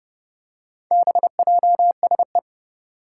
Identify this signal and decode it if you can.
Morse code, heard as a keyed tone, about 30 wpm, 700 Hz. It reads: BJSE
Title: Four Challenging Letter-Number Combinations 30wpm